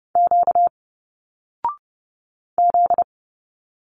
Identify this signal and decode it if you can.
Morse code, heard as a keyed tone, about 30 wpm, 700 Hz. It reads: Q Z
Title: Single Letters 30wpm